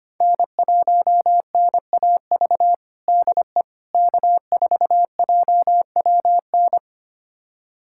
Morse code, heard as a keyed tone, about 25 wpm, 700 Hz. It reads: N1NAV DE K4JWN
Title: USA Callsign DE Callsign - 5 Characters - 25wpm